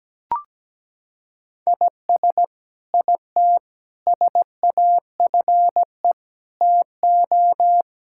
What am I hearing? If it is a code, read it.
Morse code, heard as a keyed tone, about 17 wpm, 700 Hz. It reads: IS IT SAFE TO
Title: Sentences from Top 1000 Words - Review Part 6 - 17wpm